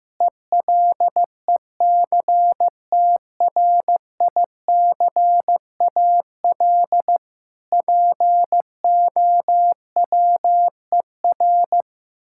Morse code, heard as a keyed tone, about 15 wpm, 700 Hz. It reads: ELECTRICAL POWER
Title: Sentences from Top 1000 Words - Review Part 6 - 15wpm